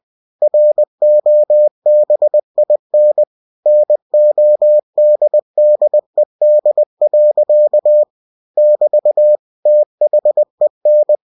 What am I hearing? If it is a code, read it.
Morse code, heard as a keyed tone, about 20 wpm, 600 Hz. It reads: ROBIN NODDED. <BT> THEN